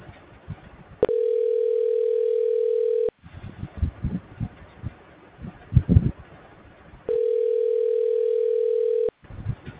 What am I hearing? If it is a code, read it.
Ringback tone (North America), 440 + 480 Hz.